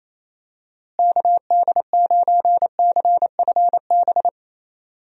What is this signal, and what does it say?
Morse code, heard as a keyed tone, about 27 wpm, 700 Hz. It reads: KD9CFB
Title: USA Call Signs - 6 Characters - Repeated 2x - 28wpm